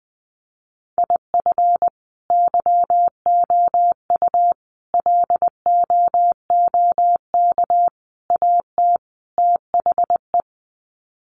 Morse code, heard as a keyed tone, about 20 wpm, 700 Hz. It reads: IF YOU LOOK AT THE
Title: Sets of 5 Words 20wpm